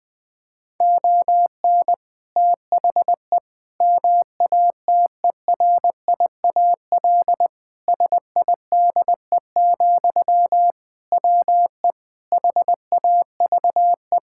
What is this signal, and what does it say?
Morse code, heard as a keyed tone, about 20 wpm, 700 Hz. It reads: ON THE MATERIAL SIDE, WE HAVE
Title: Sentences from Top 1000 Words - Review Part 5 - 20wpm